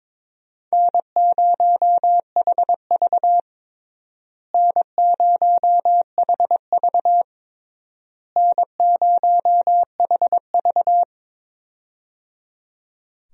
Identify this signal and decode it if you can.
Morse code, heard as a keyed tone, about 22 wpm, 700 Hz. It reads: N0HV N0HV N0HV
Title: USA Call Signs - 4 Characters - Repeated 3x - 22wpm